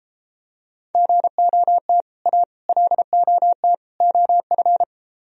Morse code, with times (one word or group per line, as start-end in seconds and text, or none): 0.9-2.0 GOT
2.3-2.4 A
2.7-3.7 LOT
4.0-4.8 OF